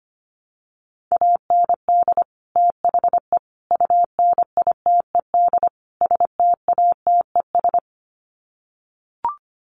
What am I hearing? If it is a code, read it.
Morse code, heard as a keyed tone, about 25 wpm, 700 Hz. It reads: AND THE UNITED STATES
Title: Sets of 4 Words 25wpm